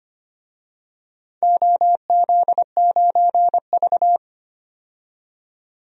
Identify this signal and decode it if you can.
Morse code, heard as a keyed tone, about 25 wpm, 700 Hz. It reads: OZ9V
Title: DXCC-100 Call Signs - 4 Characters - 25wpm